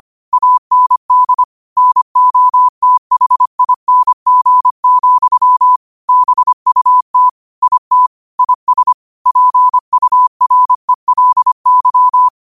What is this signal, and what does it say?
Morse code, heard as a keyed tone, about 25 wpm, 1000 Hz. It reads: AND NOTHING, BUT IT IS PURELY
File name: qotdfiles / 25wpm